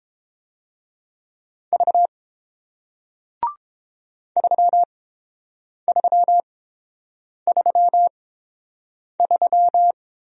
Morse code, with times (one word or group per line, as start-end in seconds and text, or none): 1.7-2.1 V
4.4-9.9 3333